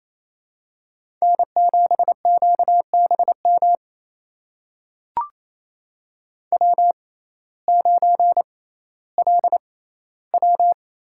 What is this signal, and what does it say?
Morse code, heard as a keyed tone, about 28 wpm, spaced as for 15 wpm, 700 Hz. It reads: N7QBM W9LW
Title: USA Call Signs - 5 Characters - 6x Character Spacing - 28wpm